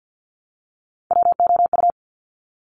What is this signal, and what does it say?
Morse code, heard as a keyed tone, about 50 wpm, 700 Hz. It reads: WOU
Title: Top Three Letter Combinations in Words 50wpm